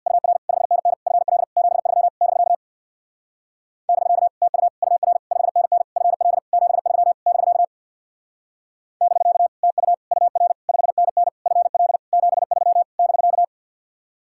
Morse code, heard as a keyed tone, about 64 wpm, 700 Hz. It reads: UR 5NN FL 73 <BK> <BK> TU UR 5NN FL 73 <BK> <BK> TU UR 5NN FL 73 <BK>